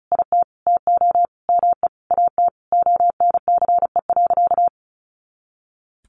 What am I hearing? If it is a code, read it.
Morse code, heard as a keyed tone, about 34 wpm, 700 Hz. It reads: IT TO ME AT ONCE.
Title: Sentences from Top 500 Words 35wpm